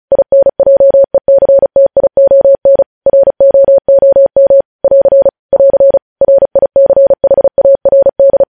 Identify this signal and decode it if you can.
Morse code, heard as a keyed tone, about 36 wpm, 550 Hz. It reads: INJECTION ROOM <AR> <AR> RICHARD
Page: News headlines in Morse code at 35 wpm